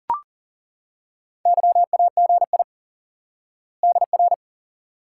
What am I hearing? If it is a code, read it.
Morse code, heard as a keyed tone, about 40 wpm, 700 Hz. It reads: YAGI DR
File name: IQE Course 3 - Lesson 89 - YAGI - Two Elements - 40wpm